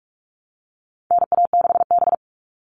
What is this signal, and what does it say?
Morse code, heard as a keyed tone, about 44 wpm, 700 Hz. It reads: NA6B
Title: All Cut Numbers - 4-Digit 45wpm